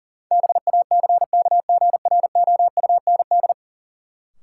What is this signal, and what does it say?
Morse code, heard as a keyed tone, about 40 wpm, 700 Hz. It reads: BACKGROUND
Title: Sentences from Top 1300 Words - New Word Drill Alt - 40wpm